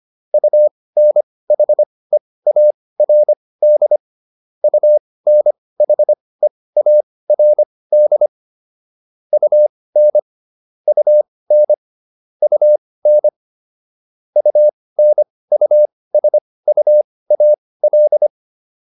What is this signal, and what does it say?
Morse code, heard as a keyed tone, about 25 wpm, spaced as for 18 wpm, 600 Hz. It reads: UNHEARD UNHEARD UN UN UN UNUSUAL